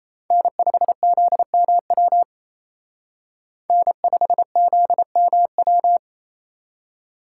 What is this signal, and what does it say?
Morse code, heard as a keyed tone, about 36 wpm, 700 Hz. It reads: N5ZMW N5ZMW